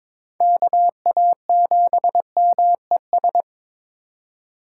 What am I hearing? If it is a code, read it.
Morse code, heard as a keyed tone, about 22 wpm, 700 Hz. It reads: KA7MES
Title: USA Call Signs - 6 Characters - 6x Character Spacing - 22wpm